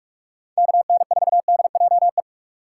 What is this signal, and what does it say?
Morse code, heard as a keyed tone, about 44 wpm, 700 Hz. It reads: KN4DJE
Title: USA Call Signs - 3x Character Spacing - 45wpm